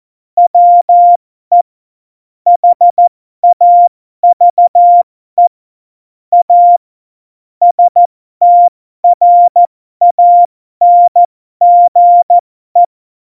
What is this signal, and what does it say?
Morse code, heard as a keyed tone, about 14 wpm, spaced as for 12 wpm, 700 Hz. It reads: WE HAVE A STRANGE
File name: Morsekurs / Witze